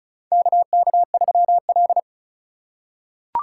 Morse code, heard as a keyed tone, about 34 wpm, 700 Hz. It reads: KK3L
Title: USA Call Signs - 4 Characters - 4x Character Spacing - 35wpm